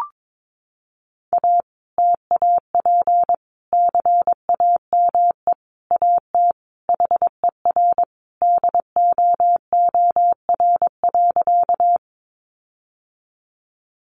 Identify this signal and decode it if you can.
Morse code, heard as a keyed tone, about 22 wpm, 700 Hz. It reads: A TAP CAME AT HER DOOR.
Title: Sentences from Top 1400 Words 22wpm